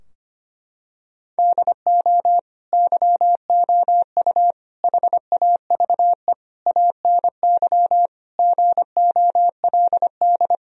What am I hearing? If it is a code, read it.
Morse code, heard as a keyed tone, about 25 wpm, 700 Hz. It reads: DO YOU HAVE ANY GOLD
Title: Sentences from Top 1000 Words - Review Part 3 - 25wpm